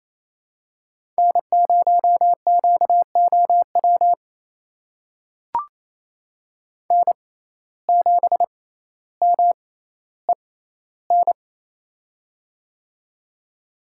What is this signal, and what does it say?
Morse code, heard as a keyed tone, about 29 wpm, spaced as for 15 wpm, 700 Hz. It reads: N0QOW N7MEN
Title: USA Call Signs - 5 Characters - 6x Character Spacing - 28wpm